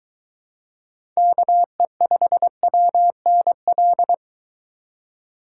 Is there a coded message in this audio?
Yes: Morse code from 1.2 to 4.1 s, heard as a keyed tone, about 23 wpm, 700 Hz. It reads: KE5WNL